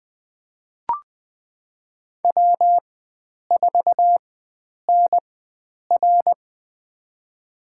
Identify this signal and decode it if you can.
Morse code, heard as a keyed tone, about 20 wpm, spaced as for 9 wpm, 700 Hz. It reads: W4NR
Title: USA Call Signs - 4 Characters - 4x Character Spacing - 20wpm